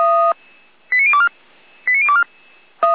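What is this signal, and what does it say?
Five-tone selective call (ZVEI), heard as quick runs of five tones, tones of about 70 ms. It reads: 89024 89024
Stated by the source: ZVEI five-tone call 89556 received by SDR.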